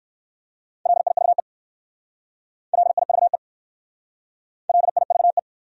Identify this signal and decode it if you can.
Morse code, heard as a keyed tone, about 58 wpm, 700 Hz. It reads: LIVE LIVE LIVE